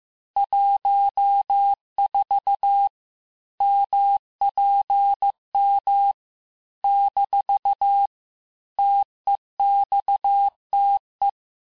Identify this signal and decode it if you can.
Morse code, heard as a keyed tone, about 15 wpm, 800 Hz. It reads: 14 MPM - TEXTE
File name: cours-du-lundi files / cours-1585-texte-chiffre-vitesse-14mpm